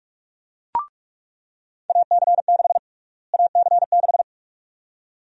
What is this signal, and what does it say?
Morse code, heard as a keyed tone, about 46 wpm, 700 Hz. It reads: AC6 AC6
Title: USA Call Sign Prefixes - Repeated 2x - 45wpm